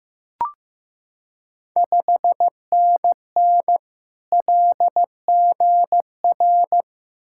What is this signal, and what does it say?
Morse code, heard as a keyed tone, about 15 wpm, 700 Hz. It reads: 5NN LGR
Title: Ontario QSO Party Exchange 15wpm